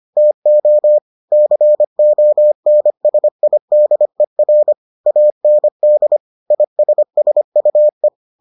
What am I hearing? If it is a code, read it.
Morse code, heard as a keyed tone, about 25 wpm, 600 Hz. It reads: TO CONSIDER AND ISSUE